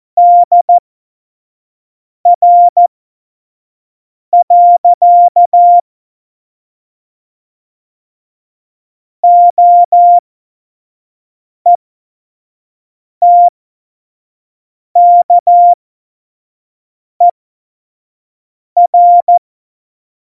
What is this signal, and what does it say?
Morse code, heard as a keyed tone, about 14 wpm, spaced as for 5 wpm, 700 Hz. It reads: DR. OETKER
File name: Morsekurs / Witze / Witz 23. April 2019, 25 F70